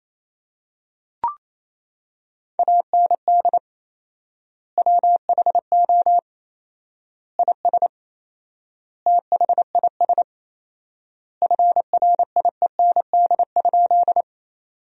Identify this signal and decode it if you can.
Morse code, heard as a keyed tone, about 28 wpm, 700 Hz. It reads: AND WHO IS THIS FRIEND?
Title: 4x Word-Spacing - Sentences from Top 300 Words - 28wpm